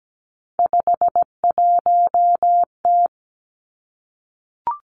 Morse code, heard as a keyed tone, about 17 wpm, 700 Hz. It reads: 51T
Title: Three Challenging Letter-Number Combinations 17wpm